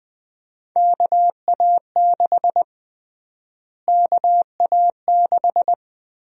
Morse code, heard as a keyed tone, about 20 wpm, 700 Hz. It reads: KA6 KA6
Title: USA Call Sign Prefixes - Repeated 2x - 20wpm